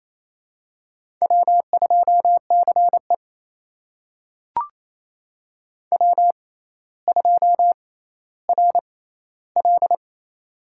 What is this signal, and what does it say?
Morse code, heard as a keyed tone, about 28 wpm, spaced as for 14 wpm, 700 Hz. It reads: W2CE W2RL
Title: USA Call Signs - 4 Characters - 6x Character Spacing - 28wpm